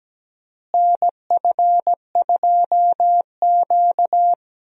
Morse code, heard as a keyed tone, about 17 wpm, 700 Hz. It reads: NF2Q
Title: USA Call Signs - 4 Characters - Repeated 3x - 17wpm